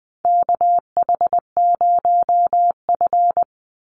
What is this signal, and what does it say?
Morse code, heard as a keyed tone, about 20 wpm, 700 Hz. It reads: KH0F
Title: USA Call Signs 20wpm